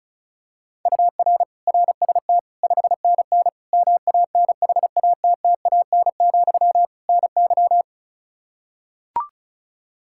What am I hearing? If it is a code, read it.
Morse code, heard as a keyed tone, about 34 wpm, 700 Hz. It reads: UR RST 5NN MANHATTAN, NY
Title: Practice QSO - Exchange 1 - Beginning - 35wpm